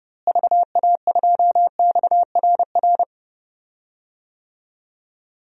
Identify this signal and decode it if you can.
Morse code, heard as a keyed tone, about 30 wpm, 700 Hz. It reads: VA2XRR
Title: DXCC-100 Call Signs - 6 Characters - Repeated 3x - 30wpm